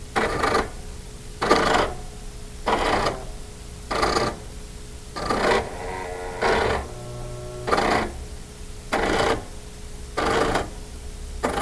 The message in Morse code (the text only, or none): none